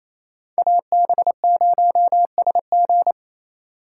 Morse code, heard as a keyed tone, about 28 wpm, 700 Hz. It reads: AB0SG
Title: USA Call Signs - 2x Character Spacing - 28wpm